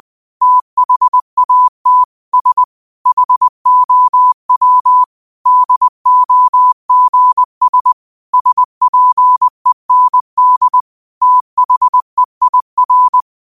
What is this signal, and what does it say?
Morse code, heard as a keyed tone, about 20 wpm, 1000 Hz. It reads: THAT S HOW DOGS SPEND THEIR
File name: 20wpm